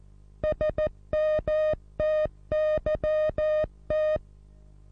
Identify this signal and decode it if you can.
Morse code, heard as a keyed tone, about 14 wpm, 620 Hz. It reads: SMTYT
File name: repos / Morsekurs / DJ4UF Morsen27